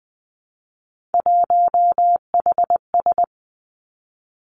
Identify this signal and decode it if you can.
Morse code, heard as a keyed tone, about 20 wpm, 700 Hz. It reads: 1HS
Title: Three Challenging Letter-Number Combinations 20wpm